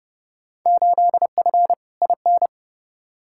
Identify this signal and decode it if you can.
Morse code, heard as a keyed tone, about 31 wpm, 700 Hz. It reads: 8F IN
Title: Field Day Exchanges Encore - 30wpm